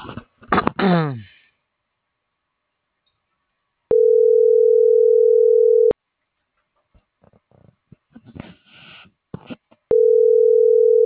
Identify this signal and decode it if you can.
Ringback tone (North America), 440 + 480 Hz.